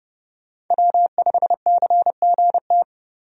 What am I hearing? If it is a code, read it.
Morse code, heard as a keyed tone, about 30 wpm, 700 Hz. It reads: W5CGT
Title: USA Call Signs - 5 Characters - 30wpm